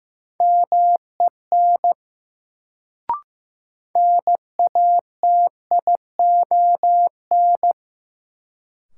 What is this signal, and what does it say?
Morse code, heard as a keyed tone, about 15 wpm, 700 Hz. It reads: MEN NATION
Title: Sets of 3 Words - New Word Drill - 15wpm